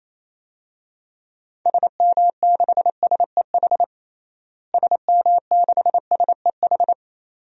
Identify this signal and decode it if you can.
Morse code, heard as a keyed tone, about 28 wpm, 700 Hz. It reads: SM6SEH SM6SEH
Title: DXCC-100 Call Signs - 6 Characters - Repeated 2x - 28wpm